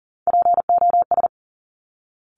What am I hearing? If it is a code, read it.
Morse code, heard as a keyed tone, about 41 wpm, 690 Hz. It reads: POS
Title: Top Three Letter Combinations in Words 40wpm